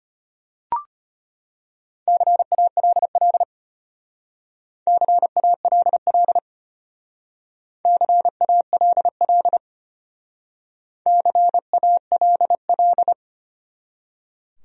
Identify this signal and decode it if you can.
Morse code, heard as a keyed tone, about 33 wpm, 700 Hz. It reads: CALL CALL CALL CALL